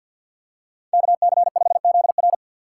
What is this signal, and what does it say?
Morse code, heard as a keyed tone, about 47 wpm, 700 Hz. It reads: KC5ZR
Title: USA Call Signs - 2x Character Spacing - 50wpm